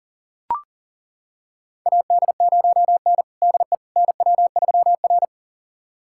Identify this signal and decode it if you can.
Morse code, heard as a keyed tone, about 38 wpm, 700 Hz. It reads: AD0N DE NW3R
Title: USA Callsign DE Callsign - 4 Characters - 40wpm